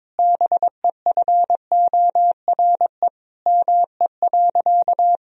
Morse code, heard as a keyed tone, about 22 wpm, 700 Hz. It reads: BEFORE ME.
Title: Sentences from Top 1000 Words - Review Part 1 - 22wpm